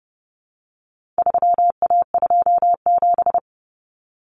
Morse code, heard as a keyed tone, about 30 wpm, 700 Hz. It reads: 3A27
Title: Common Cut Numbers - 4-Digit 30wpm